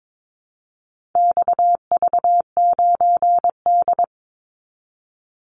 Morse code, heard as a keyed tone, about 22 wpm, 700 Hz. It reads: XV9D
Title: Call Signs - 22wpm